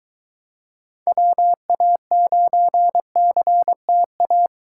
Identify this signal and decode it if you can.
Morse code, heard as a keyed tone, about 24 wpm, 700 Hz. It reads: WA9CTA